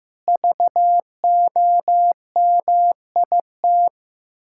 Morse code, heard as a keyed tone, about 15 wpm, 700 Hz. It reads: VOMIT
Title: Pangram Challenge v2 - 15wpm